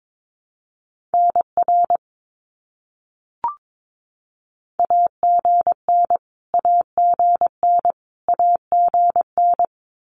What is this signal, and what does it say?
Morse code, heard as a keyed tone, about 22 wpm, 700 Hz. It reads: NR AGN AGN AGN
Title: Abbreviations Repeated 3x - 22wpm